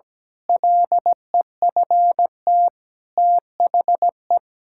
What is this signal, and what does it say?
Morse code, heard as a keyed tone, about 17 wpm, 700 Hz. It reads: LEFT THE
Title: Pangram Challenge v2 - 17wpm